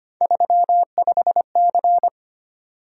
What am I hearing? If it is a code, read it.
Morse code, heard as a keyed tone, about 25 wpm, 700 Hz. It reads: 35C
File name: Three Letter-Number Combinations - 25wpm